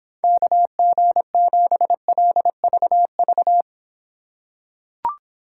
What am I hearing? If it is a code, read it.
Morse code, heard as a keyed tone, about 26 wpm, 700 Hz. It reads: KG7LVV